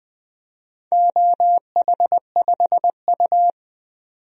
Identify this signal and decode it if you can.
Morse code, heard as a keyed tone, about 20 wpm, 700 Hz. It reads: OH5U
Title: DXCC-100 Call Signs - 4 Characters - 20wpm